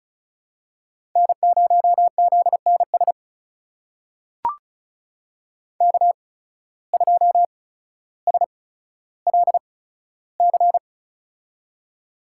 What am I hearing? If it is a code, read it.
Morse code, heard as a keyed tone, about 36 wpm, spaced as for 15 wpm, 700 Hz. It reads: N0ZNS K2SLC
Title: USA Call Signs - 5 Characters - 8x Character Spacing - 35wpm